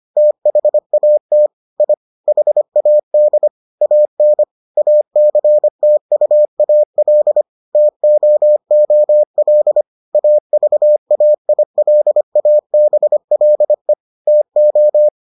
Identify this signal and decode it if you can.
Morse code, heard as a keyed tone, about 25 wpm, 600 Hz. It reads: THAT I HAD AN ACTUAL TOOL AVAILABLE TO